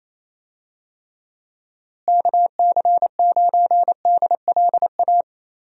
Morse code, heard as a keyed tone, about 27 wpm, 700 Hz. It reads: KC9DLA
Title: USA Call Signs - 6 Characters - Repeated 3x - 28wpm